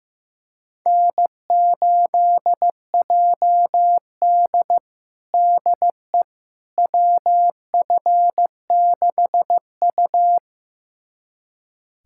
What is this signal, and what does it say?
Morse code, heard as a keyed tone, about 15 wpm, 700 Hz. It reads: N8JD DE WF6U
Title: USA Callsign DE Callsign - 4 Characters - 15wpm